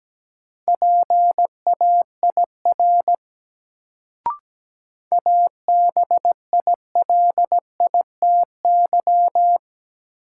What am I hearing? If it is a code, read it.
Morse code, heard as a keyed tone, about 17 wpm, 700 Hz. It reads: PAIR ABILITY
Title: Sets of 3 Words - New Word Drill - 17wpm